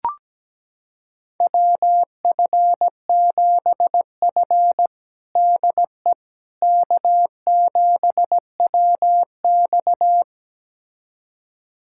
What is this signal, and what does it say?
Morse code, heard as a keyed tone, about 17 wpm, 700 Hz. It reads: WF7F DE K7WX
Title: USA Callsign DE Callsign - 4 Characters - 17wpm